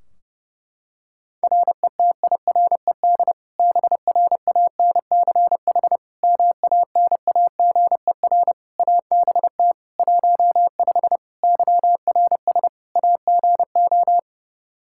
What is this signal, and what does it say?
Morse code, heard as a keyed tone, about 30 wpm, 700 Hz. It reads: RETIRED BRANCH MANAGER ABT 15 YRS AGO
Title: Practice QSO - Exchange 3 - More Info - 30wpm